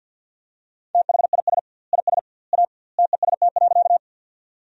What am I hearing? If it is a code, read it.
Morse code, heard as a keyed tone, about 51 wpm, 700 Hz. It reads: THIS IS A TEST.
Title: 2x Word-Spacing - Sentences from Top 500 Words - 50wpm